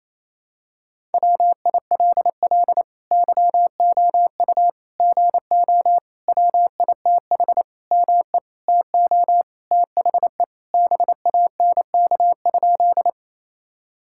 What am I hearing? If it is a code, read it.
Morse code, heard as a keyed tone, about 28 wpm, 700 Hz. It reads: WILL YOU GO WITH ME TO THE BANK?
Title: Sentences from Top 1000 Words - Review Part 6 - 28wpm